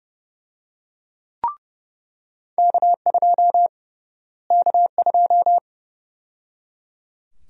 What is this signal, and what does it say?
Morse code, heard as a keyed tone, about 31 wpm, 700 Hz. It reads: K2 K2
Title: USA Call Sign Prefixes - Repeated 2x - 30wpm